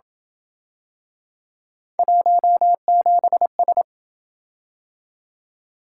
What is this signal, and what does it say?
Morse code, heard as a keyed tone, about 29 wpm, 700 Hz. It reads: 17S